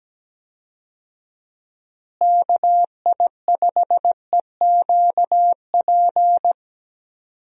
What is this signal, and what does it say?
Morse code, heard as a keyed tone, about 17 wpm, 700 Hz. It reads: KI5EQP
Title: USA Call Signs - 6 Characters - Repeated 3x - 17wpm